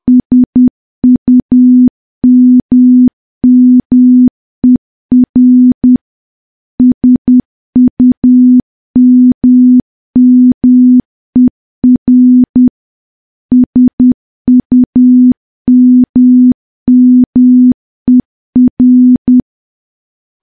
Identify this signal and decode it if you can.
Morse code, heard as a keyed tone, about 10 wpm, 250 Hz. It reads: SUMMER SUMMER SUMMER
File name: morse / morse-training-10WPM